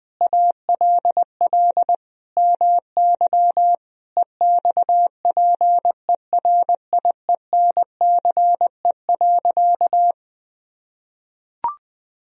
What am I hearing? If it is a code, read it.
Morse code, heard as a keyed tone, about 20 wpm, 700 Hz. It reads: ALL MY EXPERIENCE.
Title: Sentences from Top 1000 Words - Review Part 3 - 20wpm